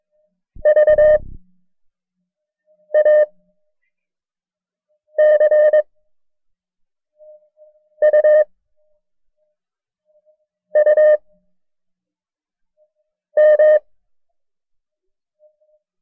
Morse code, heard as a keyed tone, about 21 wpm, spaced as for 4 wpm, 610 Hz. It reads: VACUUM
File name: FreeFiles / CW-practice / KB6MT-tape1-side2